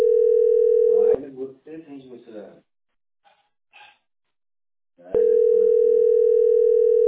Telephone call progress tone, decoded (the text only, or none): ringback tone